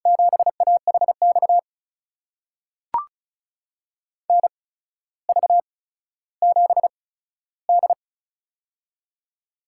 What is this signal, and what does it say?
Morse code, heard as a keyed tone, about 36 wpm, spaced as for 15 wpm, 700 Hz. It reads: 7AHX NV7D
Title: USA Call Signs - 8x Character Spacing - 35wpm